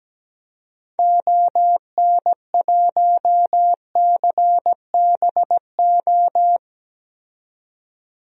Morse code, heard as a keyed tone, about 17 wpm, 700 Hz. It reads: ON1CBO
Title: DXCC-100 Call Signs - 6 Characters - 17wpm